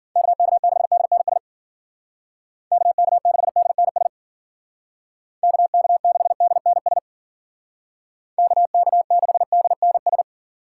Morse code, heard as a keyed tone, about 52 wpm, 700 Hz. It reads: KK6DNS KK6DNS KK6DNS KK6DNS